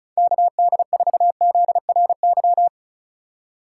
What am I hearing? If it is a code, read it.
Morse code, heard as a keyed tone, about 36 wpm, 690 Hz. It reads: KD4ZRY
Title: USA Call Signs - 6 Characters - 6x Character Spacing - 35wpm